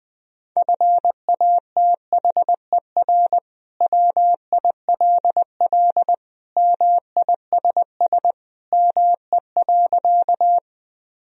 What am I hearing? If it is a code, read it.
Morse code, heard as a keyed tone, about 20 wpm, 700 Hz. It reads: FATHER WILL MISS ME.
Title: Sentences from Top 1000 Words - Review Part 3 - 20wpm